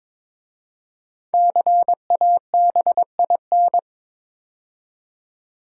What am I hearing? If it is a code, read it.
Morse code, heard as a keyed tone, about 22 wpm, 700 Hz. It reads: CABIN
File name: 5 Letter Words - First 3 Letters Given - 22wpm